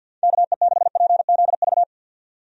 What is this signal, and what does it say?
Morse code, heard as a keyed tone, about 52 wpm, 700 Hz. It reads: KE6PZV
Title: USA Call Signs - 6 Characters - 50wpm